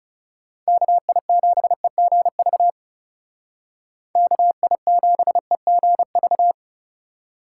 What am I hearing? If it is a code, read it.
Morse code, heard as a keyed tone, about 37 wpm, 700 Hz. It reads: KI7EGV KI7EGV
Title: Speed Racing - USA 2x3 Call Signs 30wpm